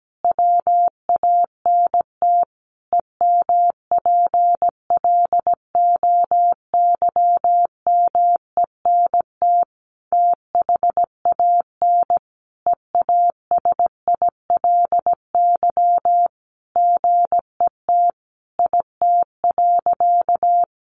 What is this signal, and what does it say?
Morse code, heard as a keyed tone, about 17 wpm, 700 Hz. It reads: WANT EMPLOYMENT THAN EASILY GET IT.
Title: Sentences from Top 1500 Words 17wpm